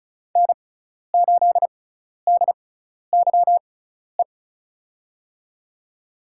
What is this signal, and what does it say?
Morse code, heard as a keyed tone, about 34 wpm, spaced as for 12 wpm, 700 Hz. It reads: N8DYE
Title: USA Call Signs - 5 Characters - 6x Character Spacing - 35wpm